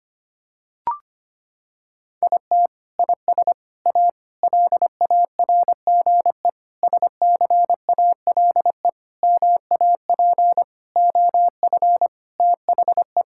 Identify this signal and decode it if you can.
Morse code, heard as a keyed tone, about 25 wpm, 700 Hz. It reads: IT IS A LARGE SCALE MAP OF THE
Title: Sentences from Top 1000 Words - Review Part 5 - 25wpm